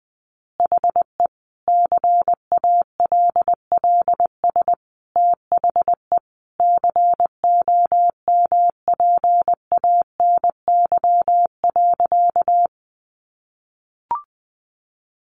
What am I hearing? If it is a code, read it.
Morse code, heard as a keyed tone, about 20 wpm, 700 Hz. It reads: HE CALLS THE COMPANY.
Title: Additional Easy Phrases 20wpm